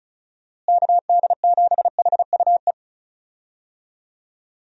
Morse code, heard as a keyed tone, about 34 wpm, 700 Hz. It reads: KD7HUE
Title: USA Call Signs - 6 Characters - Repeated 3x - 35wpm